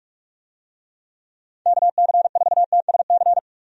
Morse code, heard as a keyed tone, about 47 wpm, 700 Hz. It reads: KC4TSC
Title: USA Call Signs - 6 Characters - 45wpm